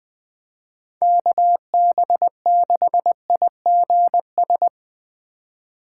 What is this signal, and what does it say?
Morse code, heard as a keyed tone, about 20 wpm, 700 Hz. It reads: KB6IGS
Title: USA Call Signs - 6 Characters - 3x Character Spacing - 20wpm